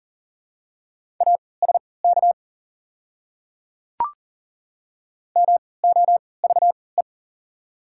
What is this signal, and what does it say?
Morse code, heard as a keyed tone, about 40 wpm, spaced as for 23 wpm, 690 Hz. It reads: ASK MOVE
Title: 3x Character Spacing - Top 300 Words - 40wpm